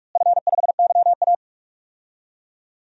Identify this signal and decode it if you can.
Morse code, heard as a keyed tone, about 45 wpm, 700 Hz. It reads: F5YA
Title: DXCC-100 Call Signs - 4 Characters - Repeated 3x - 45wpm